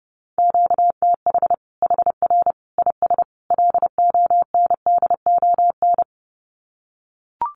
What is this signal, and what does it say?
Morse code, heard as a keyed tone, about 30 wpm, 700 Hz. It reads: QTH HR IS LONDON
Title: Common QSO Phrases 30wpm